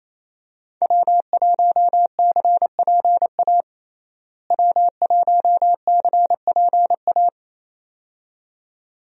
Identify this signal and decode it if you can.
Morse code, heard as a keyed tone, about 28 wpm, 700 Hz. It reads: W1CPA W1CPA
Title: USA Call Signs - 5 Characters - Repeated 2x - 28wpm